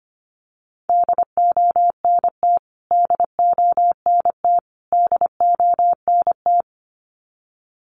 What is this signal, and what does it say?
Morse code, heard as a keyed tone, about 25 wpm, 700 Hz. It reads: DONT DONT DONT
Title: Top 300 Words Repeated 25wpm